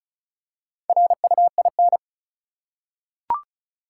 Morse code, heard as a keyed tone, about 36 wpm, 700 Hz. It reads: RUIN
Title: Four Letter Words - 35wpm